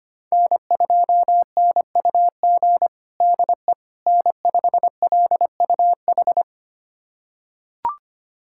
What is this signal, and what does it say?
Morse code, heard as a keyed tone, about 25 wpm, 700 Hz. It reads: N2NUG DE N5LUH
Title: USA Callsign DE Callsign - 5 Characters - 25wpm